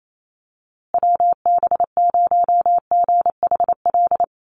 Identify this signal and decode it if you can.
Morse code, heard as a keyed tone, about 28 wpm, 700 Hz. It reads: WB0GHL
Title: USA Call Signs Repeated 2x 28wpm